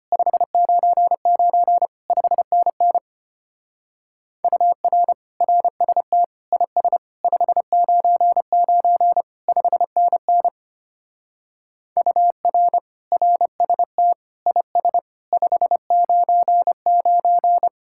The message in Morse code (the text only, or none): 599 5NN UR RST IS 599 5NN UR RST IS 599